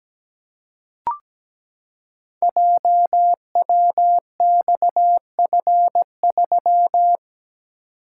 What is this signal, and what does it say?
Morse code, heard as a keyed tone, about 17 wpm, 700 Hz. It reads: JWXF3
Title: Code Groups - 5 Letter:Numbers - 17wpm